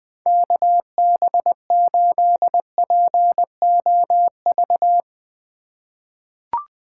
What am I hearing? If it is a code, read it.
Morse code, heard as a keyed tone, about 20 wpm, 700 Hz. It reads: KB8POV
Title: USA Call Signs - 6 Characters - 4x Character Spacing - 20wpm